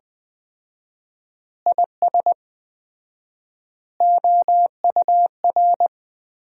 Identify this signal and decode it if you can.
Morse code, heard as a keyed tone, about 20 wpm, 700 Hz. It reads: IS OUR
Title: 4x Word-Spacing - Sentences from Top 500 Words - 20wpm